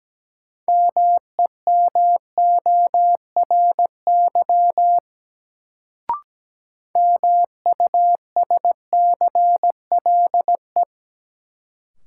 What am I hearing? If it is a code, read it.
Morse code, heard as a keyed tone, about 17 wpm, 700 Hz. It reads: MEMORY MUSCLE
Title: Sentences from Top 1300 Words - New Word Drill - 17wpm